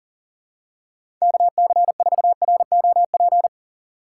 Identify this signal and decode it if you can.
Morse code, heard as a keyed tone, about 40 wpm, 700 Hz. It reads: KC4ROP
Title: USA Call Signs - 6 Characters - 3x Character Spacing - 40wpm